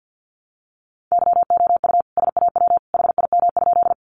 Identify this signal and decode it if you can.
Morse code, heard as a keyed tone, about 50 wpm, 700 Hz. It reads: YOU SAW HIM?
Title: Sentences from Top 300 Words 50wpm